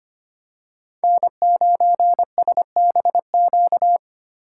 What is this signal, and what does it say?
Morse code, heard as a keyed tone, about 25 wpm, 700 Hz. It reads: N9SBQ
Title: USA Call Signs - 5 Characters - 2x Character Spacing - 25wpm